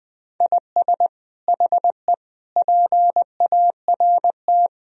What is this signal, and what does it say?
Morse code, heard as a keyed tone, about 20 wpm, 700 Hz. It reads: IS HE PART
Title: Sentences from Top 1000 Words - Review Part 2 - 20wpm